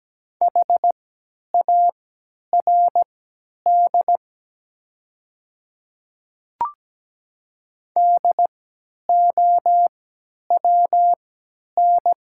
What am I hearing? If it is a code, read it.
Morse code, heard as a keyed tone, about 17 wpm, spaced as for 10 wpm, 700 Hz. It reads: HARD DOWN
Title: 3x Character Spacing - Top 300 Words - 17wpm